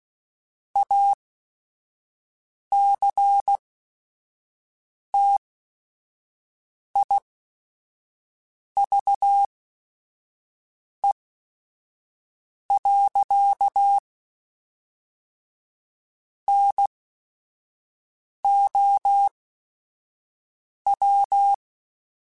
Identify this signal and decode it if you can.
Morse code, heard as a keyed tone, about 16 wpm, spaced as for 5 wpm, 800 Hz. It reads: ACTIVE. NOW
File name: Technical Subjects / Technical Subjects-mp3s / AND Gate @5